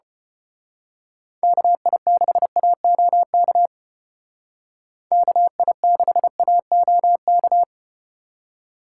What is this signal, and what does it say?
Morse code, heard as a keyed tone, about 30 wpm, 700 Hz. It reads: KI6AOK KI6AOK
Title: Speed Racing - USA 2x3 Call Signs 25wpm